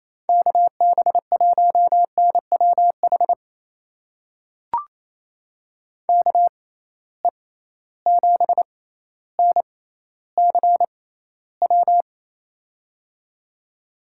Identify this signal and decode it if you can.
Morse code, heard as a keyed tone, about 28 wpm, spaced as for 14 wpm, 700 Hz. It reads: KB1NWH KE7NCW
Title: USA Call Signs - 6 Characters - 6x Character Spacing - 28wpm